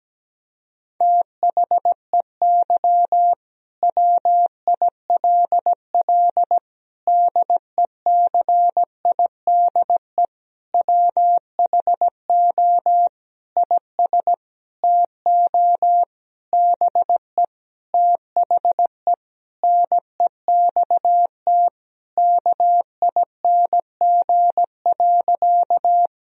Morse code, heard as a keyed tone, about 17 wpm, 700 Hz. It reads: THEY WILL DECIDE WHO IS TO BE THE NEXT KING.
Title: Sentences from Top 1000 Words - Review Part 3 - 17wpm